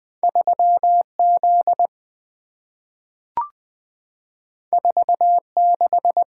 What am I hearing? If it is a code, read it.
Morse code, heard as a keyed tone, about 20 wpm, 700 Hz. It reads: 3Z 46
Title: Two Letter-Number Combinations - 20wpm